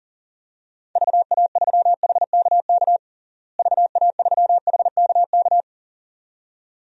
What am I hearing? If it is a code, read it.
Morse code, heard as a keyed tone, about 40 wpm, 700 Hz. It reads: VA3HKK VA3HKK
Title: DXCC-100 Call Signs - 6 Characters - Repeated 2x - 40wpm